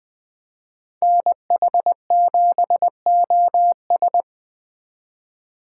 Morse code, heard as a keyed tone, about 21 wpm, 700 Hz. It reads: NH7OS